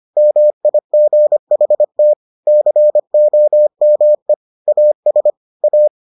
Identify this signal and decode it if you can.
Morse code, heard as a keyed tone, about 25 wpm, 600 Hz. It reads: MIGHT COME AS A